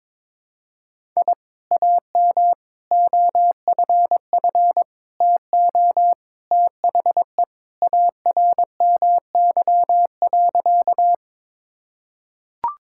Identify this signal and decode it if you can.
Morse code, heard as a keyed tone, about 22 wpm, 700 Hz. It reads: I AM OFF TO THE ARMY.
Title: Sentences from Top 1000 Words - Review Part 5 - 22wpm